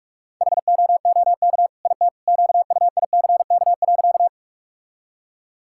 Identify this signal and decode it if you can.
Morse code, heard as a keyed tone, about 46 wpm, 700 Hz. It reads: HOOK IT QUICK.
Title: Sentences from Top 1600 Words 45wpm